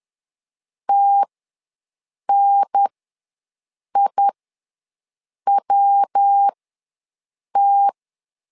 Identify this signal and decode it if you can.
Morse code, heard as a keyed tone, about 10 wpm, spaced as for 6 wpm, 800 Hz. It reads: TNIWT
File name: LEZIONI / 3 MWRO / 3 MW ws 10wpm 2213